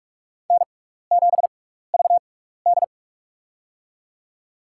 Morse code, heard as a keyed tone, about 43 wpm, spaced as for 15 wpm, 700 Hz. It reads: N7VD
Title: USA Call Signs - 4 Characters - 6x Character Spacing - 45wpm